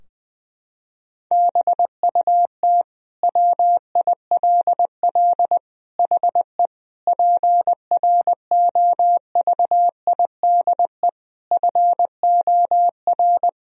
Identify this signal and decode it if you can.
Morse code, heard as a keyed tone, about 20 wpm, 700 Hz. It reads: BUT WILL HE PROVIDE FOR
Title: Sentences from Top 1000 Words - Review Part 3 - 20wpm